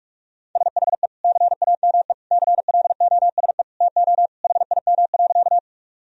Morse code, heard as a keyed tone, about 46 wpm, 700 Hz. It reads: SHE CAME CLOSE TO HIM.
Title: Sentences from Top 1000 Words - Review Part 1 - 45wpm